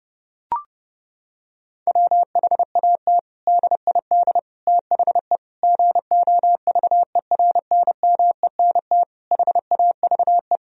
Morse code, heard as a keyed tone, about 32 wpm, 700 Hz. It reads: WHAT DID THE GOVERNMENT HAVE
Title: Sentences from Top 1000 Words - Review Part 3 - 30wpm